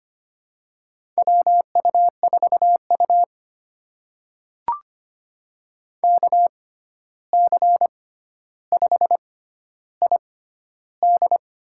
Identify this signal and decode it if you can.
Morse code, heard as a keyed tone, about 25 wpm, spaced as for 12 wpm, 700 Hz. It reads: WU4U KC5ID
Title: USA Call Signs - 6x Character Spacing - 25wpm